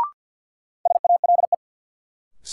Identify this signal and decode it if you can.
Morse code, heard as a keyed tone, about 52 wpm, 700 Hz. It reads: SALE
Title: Sentences from Top 1400 Words - New Word Drill Alt - 50wpm